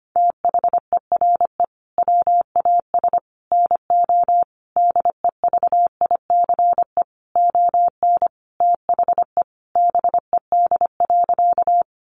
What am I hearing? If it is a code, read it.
Morse code, heard as a keyed tone, about 25 wpm, 700 Hz. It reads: THERE WAS NO DEVICE ON THE BED.
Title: Sentences from Top 1500 Words 25wpm